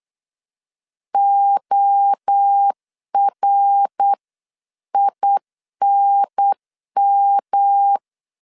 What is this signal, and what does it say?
Morse code, heard as a keyed tone, about 8 wpm, 800 Hz. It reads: ORINM
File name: LEZIONI / 6 PKX0 / 6 PKX 8wpm 3041